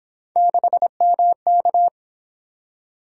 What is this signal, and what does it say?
Morse code, heard as a keyed tone, about 28 wpm, 700 Hz. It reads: 6MK